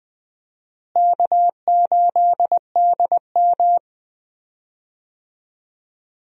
Morse code, heard as a keyed tone, about 20 wpm, 700 Hz. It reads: K8DM
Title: USA Call Signs - 4 Characters - Repeated 2x - 20wpm